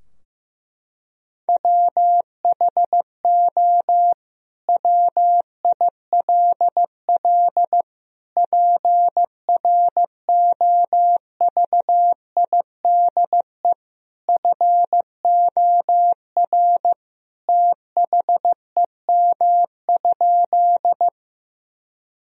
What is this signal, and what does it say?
Morse code, heard as a keyed tone, about 15 wpm, 700 Hz. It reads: WHO WILL PROVIDE FOR THEM?
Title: Sentences from Top 1000 Words - Review Part 3 - 15wpm